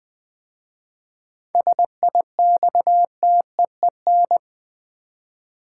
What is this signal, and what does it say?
Morse code, heard as a keyed tone, about 20 wpm, 700 Hz. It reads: SIXTEEN
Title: 7 Letter Words - First 3 Letters Given - 20wpm